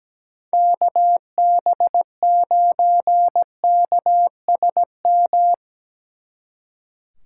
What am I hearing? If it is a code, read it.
Morse code, heard as a keyed tone, about 20 wpm, 700 Hz. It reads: KB9KSM
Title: Speed Racing - USA 2x3 Call Signs 17wpm